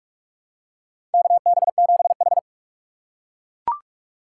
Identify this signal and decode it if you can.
Morse code, heard as a keyed tone, about 41 wpm, 700 Hz. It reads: KB7H
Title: K1USN SST QSO Practice - 45wpm